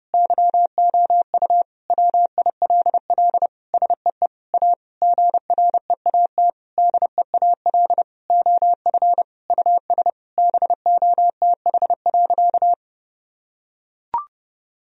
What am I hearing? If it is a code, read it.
Morse code, heard as a keyed tone, about 32 wpm, 700 Hz. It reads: YOU WILL SEE A GREAT DEAL OF US BOTH.
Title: Sentences from Top 1000 Words - Review Part 4 - 30wpm